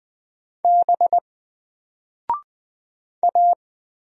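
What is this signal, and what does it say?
Morse code, heard as a keyed tone, about 20 wpm, 700 Hz. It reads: B A
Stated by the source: Single letter and number Morse drill at 20 wpm, from N4LCD.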